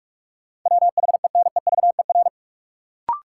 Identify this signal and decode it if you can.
Morse code, heard as a keyed tone, about 42 wpm, 700 Hz. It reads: WHENEVER
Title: Sentences from Top 1200 Words - New Word Drill Alt - 45wpm